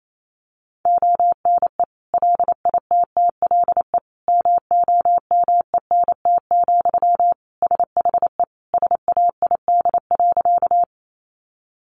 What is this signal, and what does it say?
Morse code, heard as a keyed tone, about 28 wpm, 700 Hz. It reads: ONE LITTLE MOMENT, SHE SAID.
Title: Sentences from Top 1000 Words 28wpm